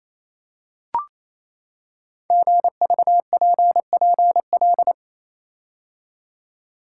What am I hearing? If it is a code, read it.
Morse code, heard as a keyed tone, about 28 wpm, 700 Hz. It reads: GVPPL
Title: Code Groups - 5 Letters - 28wpm